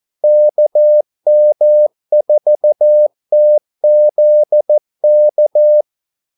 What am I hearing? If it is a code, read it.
Morse code, heard as a keyed tone, about 14 wpm, 600 Hz. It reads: KM4TZK
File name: Callsigns-14wpm